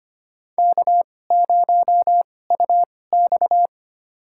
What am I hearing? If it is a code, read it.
Morse code, heard as a keyed tone, about 25 wpm, spaced as for 18 wpm, 700 Hz. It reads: K0UX
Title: USA Call Signs - 4 Characters - 2x Character Spacing - 25wpm